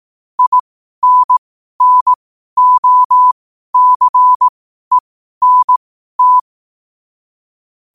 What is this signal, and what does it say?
Morse code, heard as a keyed tone, about 18 wpm, spaced as for 12 wpm, 1000 Hz. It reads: INNOCENT
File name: qotdfiles / 12wpm